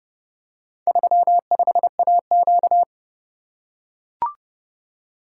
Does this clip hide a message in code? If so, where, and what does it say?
Yes: Morse code from 0.9 to 2.8 s, heard as a keyed tone, about 29 wpm, 700 Hz. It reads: 35AQ